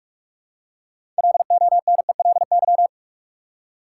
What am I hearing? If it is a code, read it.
Morse code, heard as a keyed tone, about 47 wpm, 700 Hz. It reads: LONELY
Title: Six Letter Words - 45wpm